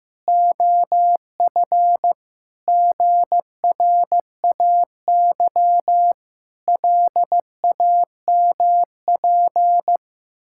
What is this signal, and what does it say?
Morse code, heard as a keyed tone, about 14 wpm, 700 Hz. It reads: OF GRAY LAMP
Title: Pangram Challenge v2 - 15wpm